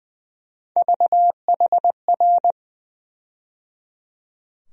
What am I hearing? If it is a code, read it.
Morse code, heard as a keyed tone, about 21 wpm, 700 Hz. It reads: VHR
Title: USA Call Sign Postfixes - 3 Letters 20wpm